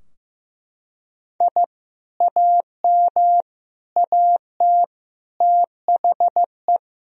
Morse code, heard as a keyed tone, about 15 wpm, 700 Hz. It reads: I AM AT THE
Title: Sentences from Top 1000 Words - Review Part 2 - 15wpm